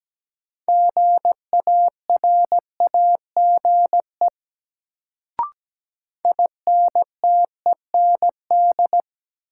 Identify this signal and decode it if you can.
Morse code, heard as a keyed tone, about 17 wpm, 700 Hz. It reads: GARAGE INTEND
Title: Sentences from Top 1600 Words - New Word Drill - 17wpm